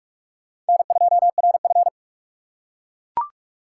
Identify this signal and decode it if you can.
Morse code, heard as a keyed tone, about 44 wpm, 700 Hz. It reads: N2RF